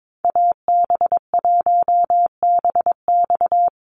Morse code, heard as a keyed tone, about 22 wpm, 700 Hz. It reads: AB1BX
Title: Call Signs Repeated 3x - 22wpm